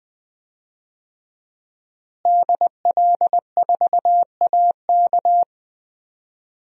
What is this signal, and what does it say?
Morse code, heard as a keyed tone, about 20 wpm, 700 Hz. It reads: DL4AK
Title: DXCC-100 Call Signs - 5 Characters - Repeated 2x - 20wpm